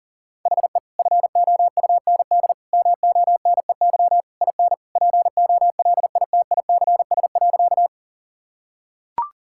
Morse code, heard as a keyed tone, about 40 wpm, 700 Hz. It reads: HE FOUND MONEY IN POLITICS.
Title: Sentences from Top 1000 Words - Review Part 6 - 40wpm